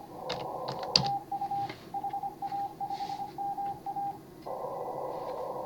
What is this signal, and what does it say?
Morse code, heard as a keyed tone, about 26 wpm, 810 Hz. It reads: TU HA5MA